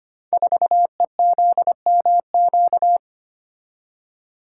Morse code, heard as a keyed tone, about 25 wpm, 700 Hz. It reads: 4EZMQ
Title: Code Groups - 5 Letter:Numbers - 25wpm